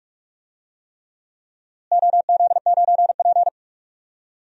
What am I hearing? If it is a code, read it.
Morse code, heard as a keyed tone, about 42 wpm, 700 Hz. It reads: OZ9P
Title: DXCC-100 Call Signs - 4 Characters - Repeated 3x - 45wpm